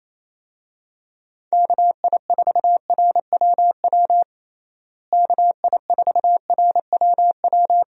Morse code, heard as a keyed tone, about 28 wpm, 700 Hz. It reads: KI4RWW KI4RWW
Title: USA Call Signs - 6 Characters - Repeated 2x - 28wpm